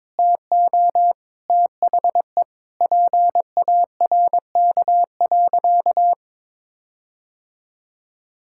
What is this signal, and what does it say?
Morse code, heard as a keyed tone, about 22 wpm, 700 Hz. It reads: TO THE PARK.
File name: practice / 22wpm / Sentences with 4-Word Phrases - 22wpm